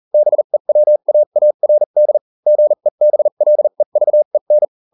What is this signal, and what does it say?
Morse code, heard as a keyed tone, about 40 wpm, spaced as for 35 wpm, 600 Hz. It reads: BEWAARD GEBLEVEN